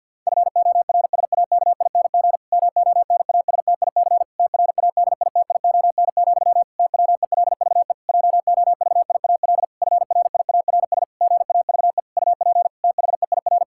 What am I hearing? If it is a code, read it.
Morse code, heard as a keyed tone, about 50 wpm, 700 Hz. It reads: FORSAKING MONASTIC TRADITION, TWELVE JOVIAL FRIARS GAVE UP THEIR
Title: Pangram Challenge v2 - 50wpm